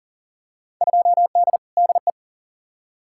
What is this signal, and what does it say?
Morse code, heard as a keyed tone, about 40 wpm, 700 Hz. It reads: 2D DE
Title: Field Day Exchanges Encore - 40wpm